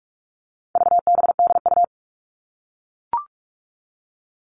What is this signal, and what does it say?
Morse code, heard as a keyed tone, about 44 wpm, 700 Hz. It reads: VBDU
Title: All Cut Numbers - 4-Digit 45wpm